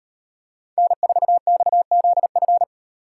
Morse code, heard as a keyed tone, about 37 wpm, 700 Hz. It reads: N4XZF